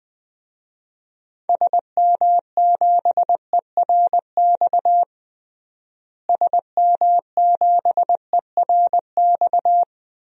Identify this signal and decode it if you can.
Morse code, heard as a keyed tone, about 20 wpm, 700 Hz. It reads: SM7ERX SM7ERX
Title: DXCC-100 Call Signs - 6 Characters - Repeated 3x - 20wpm